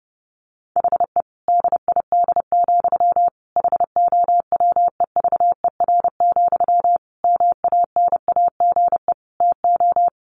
Morse code, heard as a keyed tone, about 30 wpm, 700 Hz. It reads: HE DID, HOWEVER, MANAGE TO
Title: Sentences from Top 1400 Words 30wpm